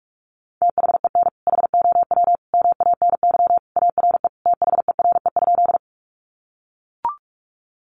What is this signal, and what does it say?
Morse code, heard as a keyed tone, about 44 wpm, 700 Hz. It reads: THEN HOW MANY ARE THERE?
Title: Sentences from Top 100 Words 45wpm